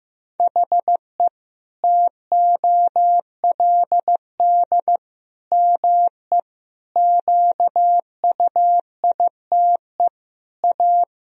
Morse code, heard as a keyed tone, about 15 wpm, 700 Hz. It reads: HE TOLD ME QUITE A
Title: Sentences from Top 1000 Words - Review Part 3 - 15wpm